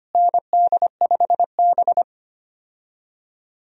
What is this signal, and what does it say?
Morse code, heard as a keyed tone, about 25 wpm, 700 Hz. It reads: ND5B
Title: USA Call Signs - 4 Characters - Repeated 3x - 25wpm